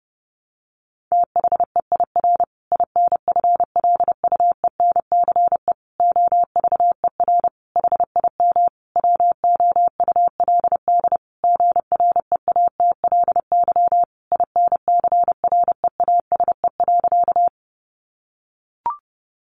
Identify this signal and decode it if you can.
Morse code, heard as a keyed tone, about 30 wpm, 700 Hz. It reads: THEIR INFLUENCE OVER HIM WOULD GREATLY INCREASE.
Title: Sentences from Top 1400 Words 30wpm